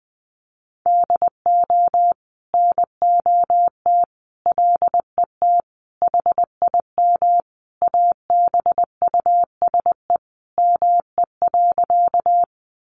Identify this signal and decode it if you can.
Morse code, heard as a keyed tone, about 19 wpm, 700 Hz. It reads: DO NOT LET HIM ABUSE ME.
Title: Sentences from Top 1300 Words 20wpm